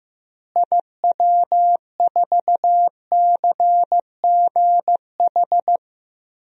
Morse code, heard as a keyed tone, about 15 wpm, 700 Hz. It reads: IW4CGH
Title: DXCC-100 Call Signs - 6 Characters - Repeated 3x - 15wpm